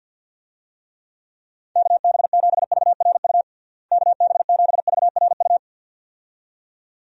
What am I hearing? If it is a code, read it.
Morse code, heard as a keyed tone, about 50 wpm, 700 Hz. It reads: KB7VRU KB7VRU
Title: USA Call Signs - 6 Characters - Repeated 2x - 50wpm